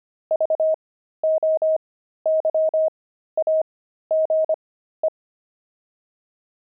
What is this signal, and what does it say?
Morse code, heard as a keyed tone, about 25 wpm, spaced as for 13 wpm, 630 Hz. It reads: VOYAGE
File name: pangram-fs-13-cs-25-blimp